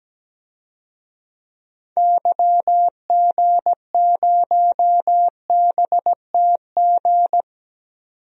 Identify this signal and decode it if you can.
Morse code, heard as a keyed tone, about 17 wpm, 700 Hz. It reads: YG0BTG
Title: DXCC-100 Call Signs - 6 Characters - 17wpm